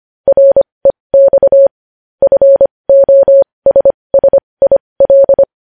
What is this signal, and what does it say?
Morse code, heard as a keyed tone, about 25 wpm, spaced as for 20 wpm, 550 Hz. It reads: REX FOSSIL
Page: news headlines in morse code at 20 wpm